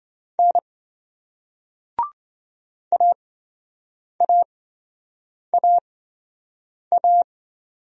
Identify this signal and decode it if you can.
Morse code, heard as a keyed tone, about 27 wpm, spaced as for 7 wpm, 700 Hz. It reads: N AAAA